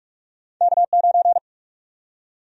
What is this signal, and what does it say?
Morse code, heard as a keyed tone, about 46 wpm, 700 Hz. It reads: K9
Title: USA Call Sign Prefixes - 45wpm